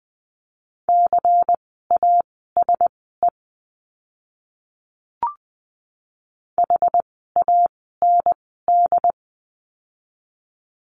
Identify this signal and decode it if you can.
Morse code, heard as a keyed tone, about 20 wpm, spaced as for 14 wpm, 700 Hz. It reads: CASE HAND
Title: 206 Lesson 33f - Letter j - Words 20wpm